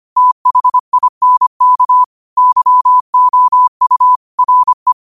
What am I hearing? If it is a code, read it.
Morse code, heard as a keyed tone, about 25 wpm, 1000 Hz. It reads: THINK YOU RE
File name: qotdfiles / 25wpm